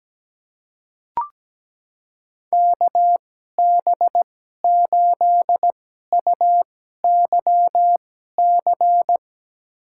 Morse code, heard as a keyed tone, about 17 wpm, spaced as for 12 wpm, 700 Hz. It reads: KB8UYC
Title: USA Call Signs - 2x Character Spacing - 17wpm